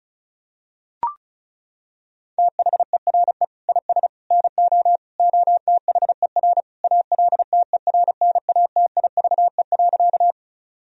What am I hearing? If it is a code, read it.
Morse code, heard as a keyed tone, about 34 wpm, 700 Hz. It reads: THERE IS NO OTHER ALTERNATIVE.
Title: Sentences from Top 1000 Words - Review Part 4 - 35wpm